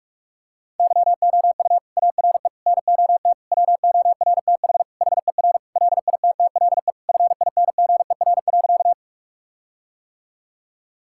Morse code, heard as a keyed tone, about 43 wpm, 700 Hz. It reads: YOU ARE NOT WORTH HER LITTLE FINGER.
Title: Sentences from Top 1600 Words 45wpm